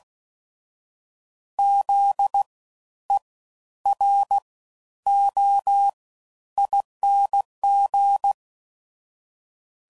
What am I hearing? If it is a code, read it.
Morse code, heard as a keyed tone, about 16 wpm, spaced as for 10 wpm, 800 Hz. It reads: ZEROING
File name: Suffixes / Special INGs / Fast Ings